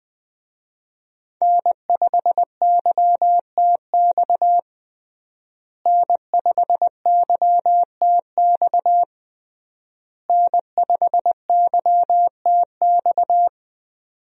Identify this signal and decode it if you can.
Morse code, heard as a keyed tone, about 20 wpm, 700 Hz. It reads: N5YTX N5YTX N5YTX
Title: USA Call Signs - 5 Characters - Repeated 3x - 20wpm